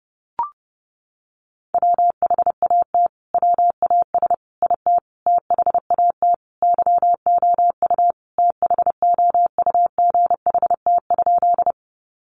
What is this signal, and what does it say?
Morse code, heard as a keyed tone, about 30 wpm, 700 Hz. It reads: WHAT WAS IT THAT YOU THOUGHT?
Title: Sentences from Top 300 Words 30wpm